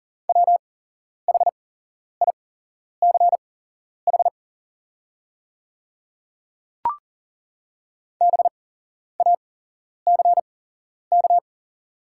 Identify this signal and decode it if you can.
Morse code, heard as a keyed tone, about 40 wpm, spaced as for 11 wpm, 700 Hz. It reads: WHICH BACK
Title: 8x Character Spacing - Top 300 Words - 40wpm